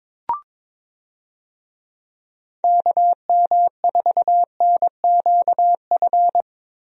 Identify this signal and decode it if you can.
Morse code, heard as a keyed tone, about 21 wpm, 700 Hz. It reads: KM4NQF
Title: USA Call Signs - 6 Characters - Repeated 3x - 22wpm